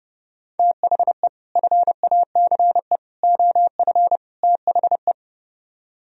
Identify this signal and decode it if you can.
Morse code, heard as a keyed tone, about 36 wpm, 700 Hz. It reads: THE FACE OF THE
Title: Speed-Racing - Sets of 4 Words 30wpm